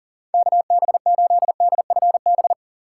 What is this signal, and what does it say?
Morse code, heard as a keyed tone, about 40 wpm, 700 Hz. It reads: KB8DFB
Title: USA Call Signs - 6 Characters - 2x Character Spacing - 40wpm